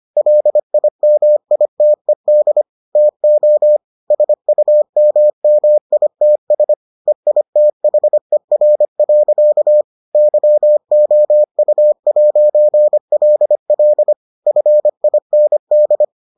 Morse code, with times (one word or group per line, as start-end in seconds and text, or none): 0.2-2.6 LIMITED
3.0-3.8 TO
4.1-6.7 SUMMITS
7.1-9.8 EITHER.
10.2-14.1 YOU'LL
14.5-16.0 FIND